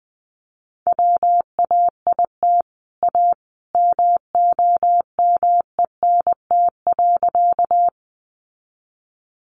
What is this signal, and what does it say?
Morse code, heard as a keyed tone, about 20 wpm, 700 Hz. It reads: WAIT A MOMENT.
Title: Sentences from Top 1000 Words 20wpm